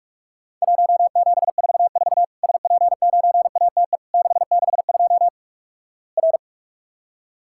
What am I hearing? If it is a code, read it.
Morse code, heard as a keyed tone, about 48 wpm, 700 Hz. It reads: 1744 SP9ATE 662 R
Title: Worked All Europe DX Contest - QTC Exchange - 45wpm